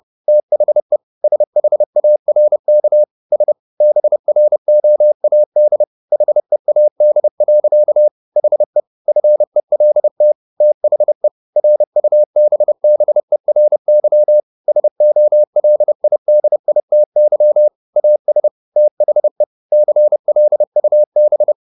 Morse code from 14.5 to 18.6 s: SOLIDITY AS

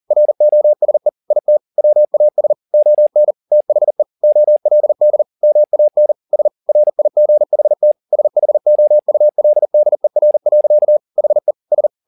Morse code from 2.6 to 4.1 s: ON THE

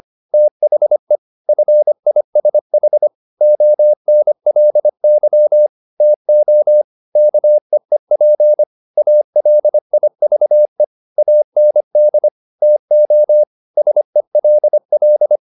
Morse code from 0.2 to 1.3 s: THE

